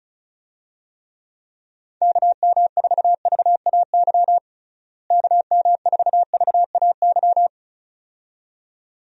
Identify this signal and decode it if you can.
Morse code, heard as a keyed tone, about 36 wpm, 700 Hz. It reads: KM4VAY KM4VAY
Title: USA Call Signs - 6 Characters - Repeated 2x - 35wpm